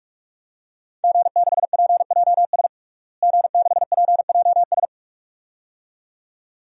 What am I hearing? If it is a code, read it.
Morse code, heard as a keyed tone, about 43 wpm, 700 Hz. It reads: G6PJS G6PJS
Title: DXCC-100 Call Signs - 5 Characters - Repeated 2x - 45wpm